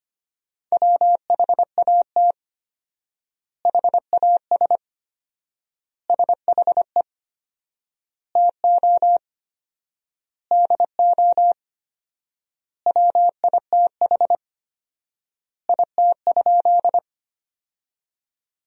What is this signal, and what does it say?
Morse code, heard as a keyed tone, about 25 wpm, 700 Hz. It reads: WHAT HAS SHE TO DO WITH IT?
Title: 4x Word-Spacing - Sentences from Top 100 Words - 25wpm